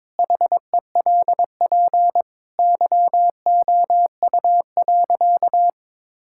Morse code, heard as a keyed tone, about 22 wpm, 700 Hz. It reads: HELP YOU.
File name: practice / 22wpm / Sentences with 4-Word Phrases - 22wpm